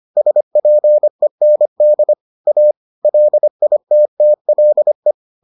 Morse code, heard as a keyed tone, about 25 wpm, 600 Hz. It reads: SPEND A LITTLE